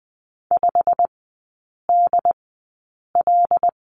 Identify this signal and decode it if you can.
Morse code, heard as a keyed tone, about 20 wpm, spaced as for 8 wpm, 700 Hz. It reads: 5DL
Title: 100 Lesson 17e - Letter u - Three Characters 20wpm